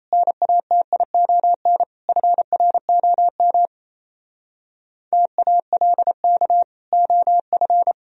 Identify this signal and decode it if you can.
Morse code, heard as a keyed tone, about 37 wpm, 700 Hz. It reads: NATION FROM TALK OF